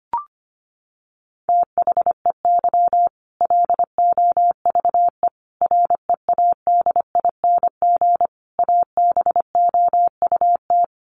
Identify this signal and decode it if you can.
Morse code, heard as a keyed tone, about 25 wpm, 700 Hz. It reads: THEY LOVE READING ABOUT
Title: Sentences from Top 600 Words 25wpm